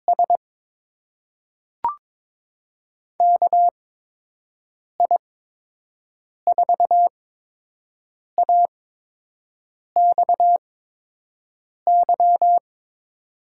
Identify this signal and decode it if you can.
Morse code, heard as a keyed tone, about 22 wpm, spaced as for 6 wpm, 700 Hz. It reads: S KI4AXY
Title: USA Call Signs - 8x Character Spacing - 22wpm